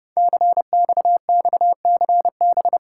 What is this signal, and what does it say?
Morse code, heard as a keyed tone, about 30 wpm, 700 Hz. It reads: CXXCB
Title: Code Groups - 5 Letters - 30wpm